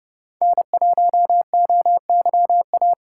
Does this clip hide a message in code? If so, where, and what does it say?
Yes: Morse code from 0.4 to 2.9 s, heard as a keyed tone, about 34 wpm, 700 Hz. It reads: N1OYA